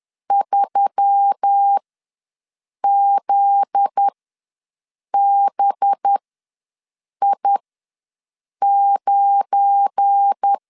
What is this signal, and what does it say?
Morse code, heard as a keyed tone, about 10 wpm, spaced as for 6 wpm, 800 Hz. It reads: 3ZBI9
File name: LEZIONI / 7 GF9Z / 7 GF9Z ws 10wpm 3919